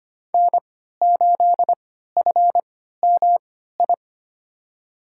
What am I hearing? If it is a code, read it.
Morse code, heard as a keyed tone, about 25 wpm, spaced as for 14 wpm, 700 Hz. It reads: N8FMI
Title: USA Call Signs - 5 Characters - 3x Character Spacing - 25wpm